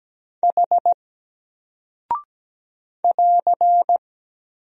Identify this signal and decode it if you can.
Morse code, heard as a keyed tone, about 17 wpm, 700 Hz. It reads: H <AR>
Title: Single Character (Letter:Number:Punctuation:Prosign) 17wpm